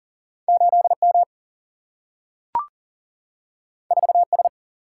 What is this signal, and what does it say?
Morse code, heard as a keyed tone, about 39 wpm, 700 Hz. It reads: 8 M 4 S
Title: Single-Letter Single-Number 40wpm